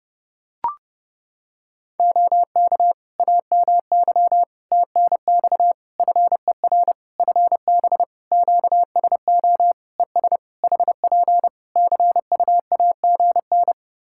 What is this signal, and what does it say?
Morse code, heard as a keyed tone, about 31 wpm, 700 Hz. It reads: OK AMY TNX FER FB QSO ES HP CUAGN
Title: Practice QSO - Exchange 4 - Ending - 30wpm